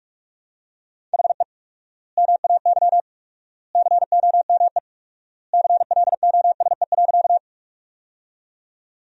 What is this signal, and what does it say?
Morse code, heard as a keyed tone, about 43 wpm, 700 Hz. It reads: HE MAY COME CLOSE.
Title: 4x Word-Spacing - Sentences from Top 300 Words - 45wpm